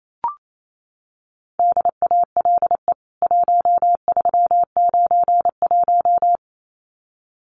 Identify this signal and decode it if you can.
Morse code, heard as a keyed tone, about 28 wpm, 700 Hz. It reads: DALE 1391
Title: CWT Exchange 28wpm